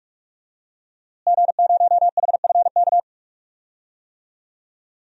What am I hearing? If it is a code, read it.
Morse code, heard as a keyed tone, about 43 wpm, 700 Hz. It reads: G0HFK
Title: DXCC-100 Call Signs - 5 Characters - Repeated 3x - 45wpm